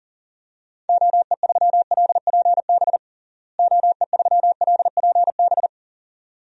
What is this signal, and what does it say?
Morse code, heard as a keyed tone, about 40 wpm, 700 Hz. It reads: OE3LPB OE3LPB
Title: DXCC-100 Call Signs - 6 Characters - Repeated 2x - 40wpm